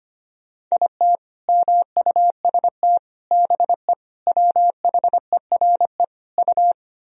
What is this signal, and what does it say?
Morse code, heard as a keyed tone, about 25 wpm, 700 Hz. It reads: IT MUST BE WHERE U
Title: Sentences from Top 200 Words and Abbreviations 25wpm